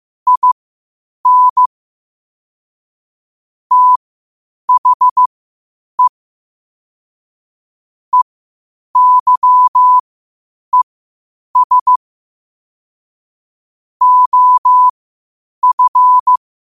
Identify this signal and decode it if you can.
Morse code, heard as a keyed tone, about 15 wpm, spaced as for 8 wpm, 1000 Hz. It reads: IN THE EYES OF
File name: qotdfiles / 8wpm